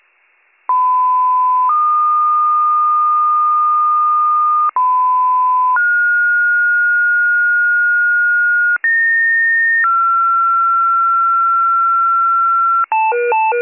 Two-tone page, heard as a one-second tone, then a three-second tone, about 1010 then 1220 Hz, about 1010 then 1430 Hz, about 1850 then 1360 Hz, with noise. It is heard from 0.7 to 12.9 s.